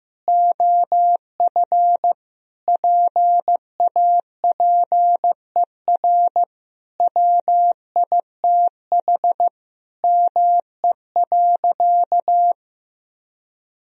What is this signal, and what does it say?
Morse code, heard as a keyed tone, about 15 wpm, 700 Hz. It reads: OF PAPER WITH ME.
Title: Sentences from Top 1000 Words - Review Part 2 - 15wpm